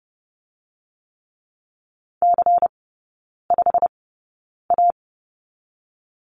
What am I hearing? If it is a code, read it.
Morse code, heard as a keyed tone, about 30 wpm, 700 Hz. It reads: C 5 A
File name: Lesson 19e - Letter w - Three Characters 30wpm